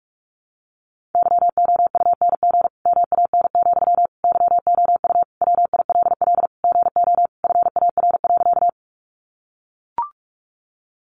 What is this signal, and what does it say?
Morse code, heard as a keyed tone, about 45 wpm, 700 Hz. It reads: YOUNG MAN, YOU WILL GO FAR.
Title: Sentences from Top 400 Words 45wpm